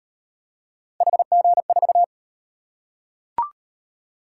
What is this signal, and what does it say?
Morse code, heard as a keyed tone, about 38 wpm, 700 Hz. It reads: HG4